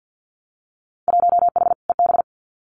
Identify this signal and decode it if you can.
Morse code, heard as a keyed tone, about 50 wpm, 700 Hz. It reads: 1H EB
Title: Winter Field Day Exchanges Encore 50wpm